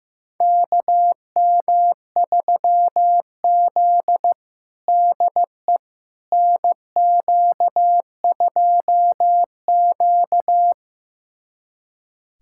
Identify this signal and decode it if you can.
Morse code, heard as a keyed tone, about 15 wpm, 700 Hz. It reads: KM3Z DE NQ2Q
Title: USA Callsign DE Callsign - 4 Characters - 15wpm